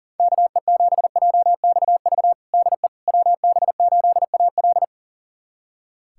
Morse code, heard as a keyed tone, about 40 wpm, 700 Hz. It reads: KE7JXV DE WB8AL
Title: USA Callsign DE Callsign 40wpm